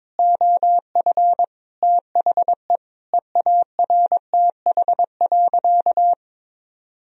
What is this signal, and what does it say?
Morse code, heard as a keyed tone, about 22 wpm, 700 Hz. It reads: OF THE EARTH.
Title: Sentences from Top 1000 Words - Review Part 2 - 22wpm